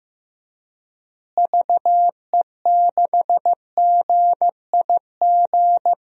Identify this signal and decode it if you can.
Morse code, heard as a keyed tone, about 15 wpm, 700 Hz. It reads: VE6GIG
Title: DXCC-100 Call Signs - 6 Characters - Repeated 3x - 15wpm